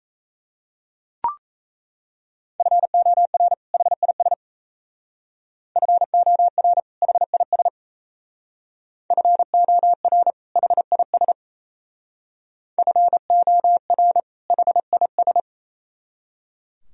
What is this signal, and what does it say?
Morse code, heard as a keyed tone, about 37 wpm, 700 Hz. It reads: FOR HIS FOR HIS FOR HIS FOR HIS